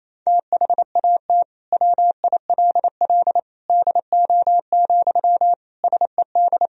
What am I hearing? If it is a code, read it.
Morse code, heard as a keyed tone, about 28 wpm, 700 Hz. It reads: THAT WILL DO, SED
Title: Sentences from Top 100 Words and Abbreviations 28wpm